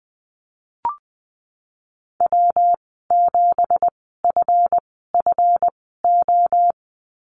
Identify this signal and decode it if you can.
Morse code, heard as a keyed tone, about 20 wpm, spaced as for 15 wpm, 700 Hz. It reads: W7FFO
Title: 179 Lesson 29g - Slash - Call Signs 20wpm